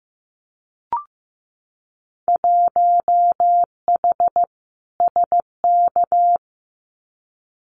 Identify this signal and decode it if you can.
Morse code, heard as a keyed tone, about 15 wpm, 700 Hz. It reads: 1H SK
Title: Winter Field Day Exchanges 15wpm